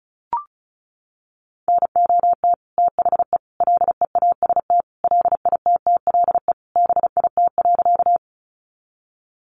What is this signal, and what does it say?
Morse code, heard as a keyed tone, about 34 wpm, 700 Hz. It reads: NOT THE LEAST LITTLE BIT.
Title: Sentences from Top 900 Words 35wpm